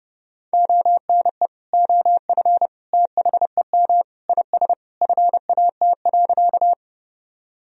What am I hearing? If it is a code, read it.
Morse code, heard as a keyed tone, about 32 wpm, 700 Hz. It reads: ONE OF THEM IS FAT.
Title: Sentences from Top 1000 Words - Review Part 4 - 30wpm